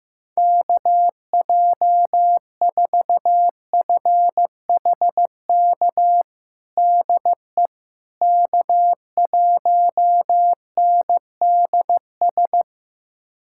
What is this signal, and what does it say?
Morse code, heard as a keyed tone, about 15 wpm, 700 Hz. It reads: KJ4FHK DE K1NDS
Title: USA Callsign DE Callsign 15wpm